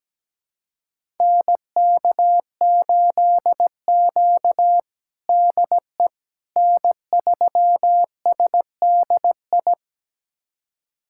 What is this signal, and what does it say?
Morse code, heard as a keyed tone, about 17 wpm, 700 Hz. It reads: NK8Q DE N3SDI
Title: USA Callsign DE Callsign 17wpm